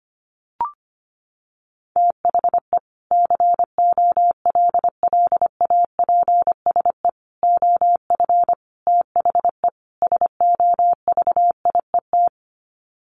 Morse code, heard as a keyed tone, about 25 wpm, 700 Hz. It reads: THE COLLAPSE OF THE SOVIET
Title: Sets of 5 Words Encore 25wpm